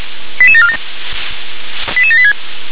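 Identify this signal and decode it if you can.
Five-tone selective call (ZVEI), heard as quick runs of five tones, tones of about 70 ms. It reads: 88747 88786
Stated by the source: Five-tone ZVEI selective call 88164, received by SDR.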